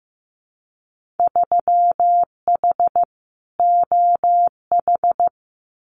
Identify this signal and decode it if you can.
Morse code, heard as a keyed tone, about 15 wpm, 700 Hz. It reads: 3H OH
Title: Winter Field Day Exchanges Encore 15wpm